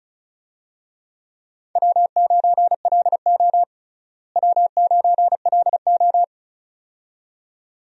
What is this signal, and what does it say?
Morse code, heard as a keyed tone, about 36 wpm, 700 Hz. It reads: W9LO W9LO
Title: USA Call Signs - 4 Characters - Repeated 2x - 35wpm